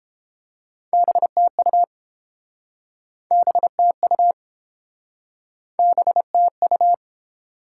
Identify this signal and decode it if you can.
Morse code, heard as a keyed tone, about 30 wpm, 700 Hz. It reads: BTU BTU BTU